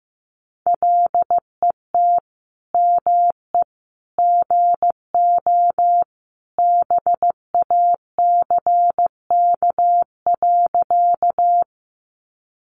Morse code, heard as a keyed tone, about 15 wpm, 700 Hz. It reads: LET ME GO BACK.
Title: Sentences from Top 300 Words 15wpm